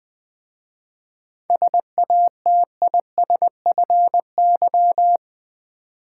Morse code, heard as a keyed tone, about 20 wpm, 690 Hz. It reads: SATISFY
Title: 7 Letter Words - First 3 Letters Given - 20wpm